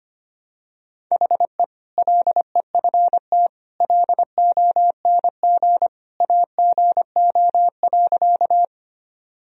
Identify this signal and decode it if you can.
Morse code, heard as a keyed tone, about 25 wpm, 700 Hz. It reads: HE LEFT LONG AGO.
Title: Sentences from Top 1000 Words - Review Part 3 - 25wpm